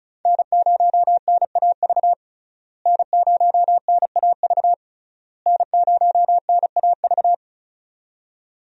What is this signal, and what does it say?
Morse code, heard as a keyed tone, about 36 wpm, 700 Hz. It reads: N0NAV N0NAV N0NAV
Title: USA Call Signs - 5 Characters - Repeated 3x - 35wpm